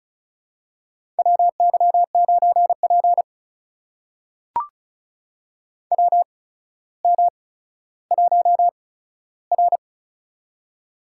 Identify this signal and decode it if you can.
Morse code, heard as a keyed tone, about 36 wpm, spaced as for 15 wpm, 700 Hz. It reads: WY9P WM1R
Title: USA Call Signs - 4 Characters - 8x Character Spacing - 35wpm